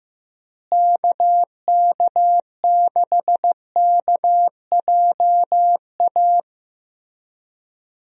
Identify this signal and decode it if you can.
Morse code, heard as a keyed tone, about 15 wpm, 700 Hz. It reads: KK6KJA
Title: USA Call Signs - 6 Characters - Repeated 3x - 15wpm